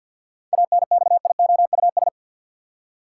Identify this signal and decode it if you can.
Morse code, heard as a keyed tone, about 45 wpm, 700 Hz. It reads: ANXIOUS
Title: Seven Letter Words - 50wpm